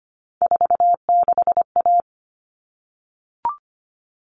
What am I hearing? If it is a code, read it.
Morse code, heard as a keyed tone, about 25 wpm, 700 Hz. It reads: 46A
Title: All Cut Numbers - 3-Digit 25wpm